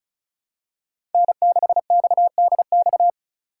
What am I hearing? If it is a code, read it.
Morse code, heard as a keyed tone, about 34 wpm, 700 Hz. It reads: N6XDX
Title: USA Call Signs - 2x Character Spacing - 35wpm